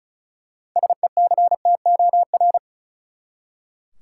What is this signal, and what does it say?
Morse code, heard as a keyed tone, about 36 wpm, 700 Hz. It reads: SECTOR
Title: Six Letter Words - 35wpm